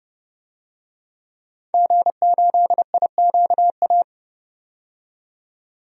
Morse code, heard as a keyed tone, about 30 wpm, 700 Hz. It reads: G8IQA
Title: DXCC-100 Call Signs - 5 Characters - 30wpm